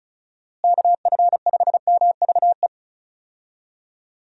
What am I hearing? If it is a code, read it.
Morse code, heard as a keyed tone, about 34 wpm, 700 Hz. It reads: KF5MVE
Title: USA Call Signs - 6 Characters - Repeated 3x - 35wpm